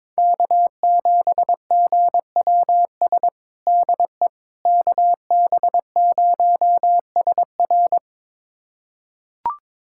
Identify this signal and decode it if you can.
Morse code, heard as a keyed tone, about 21 wpm, 700 Hz. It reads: K7GWS DE KB0SR
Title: USA Callsign DE Callsign - 5 Characters - 22wpm